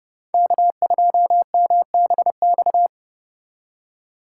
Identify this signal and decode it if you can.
Morse code, heard as a keyed tone, about 29 wpm, 700 Hz. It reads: K2MBX